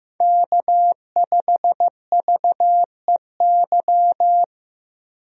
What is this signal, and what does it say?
Morse code, heard as a keyed tone, about 15 wpm, 700 Hz. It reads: K5VEY
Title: USA Call Signs - 5 Characters - 15wpm